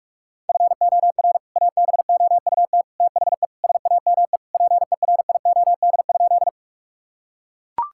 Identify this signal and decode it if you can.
Morse code, heard as a keyed tone, about 43 wpm, 700 Hz. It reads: FOR ABOUT THE SAME PERIOD?
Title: Sentences from Top 1000 Words - Review Part 5 - 45wpm